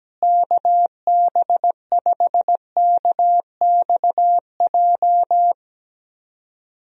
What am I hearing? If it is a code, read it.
Morse code, heard as a keyed tone, about 17 wpm, 700 Hz. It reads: KB5KXJ
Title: USA Call Signs - 6 Characters - Repeated 3x - 17wpm